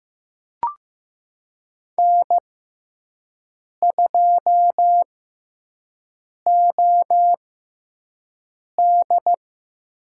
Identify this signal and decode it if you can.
Morse code, heard as a keyed tone, about 15 wpm, spaced as for 5 wpm, 700 Hz. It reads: N2OD
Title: USA Call Signs - 6x Character Spacing - 15wpm